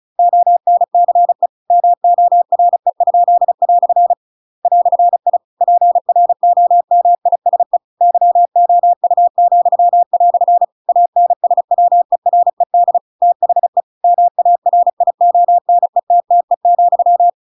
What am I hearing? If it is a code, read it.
Morse code, heard as a keyed tone, about 34 wpm, 700 Hz. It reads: ONCE MORE?" "I PROMISE YOU," ANSWERED THE MARIONETTE,
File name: Pinocchio / Pinocchio ch8 35 35 ik1hge